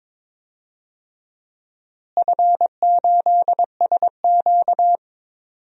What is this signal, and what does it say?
Morse code, heard as a keyed tone, about 21 wpm, 700 Hz. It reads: F8SQ
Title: DXCC-100 Call Signs - 4 Characters - Repeated 2x - 22wpm